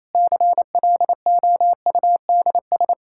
Morse code, heard as a keyed tone, about 27 wpm, 700 Hz. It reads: CLOUDS
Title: Practice QSO - Exchange 2 - Info - 28wpm